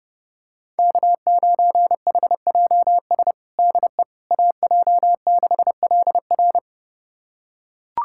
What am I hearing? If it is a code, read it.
Morse code, heard as a keyed tone, about 30 wpm, 700 Hz. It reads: K9HJS DE AJ6LR
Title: USA Callsign DE Callsign - 5 Characters - 30wpm